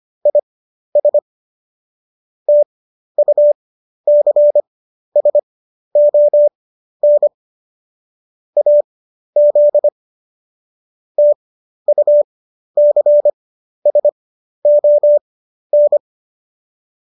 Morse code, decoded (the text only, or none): IS TUCSON AZ TUCSON